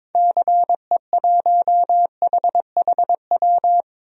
Morse code, heard as a keyed tone, about 22 wpm, 700 Hz. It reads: CE1HHW
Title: DXCC-100 Call Signs - 6 Characters - Repeated 3x - 22wpm